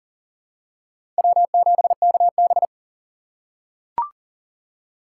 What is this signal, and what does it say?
Morse code, heard as a keyed tone, about 40 wpm, 700 Hz. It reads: W7KB
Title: USA Call Signs - 2x Character Spacing - 40wpm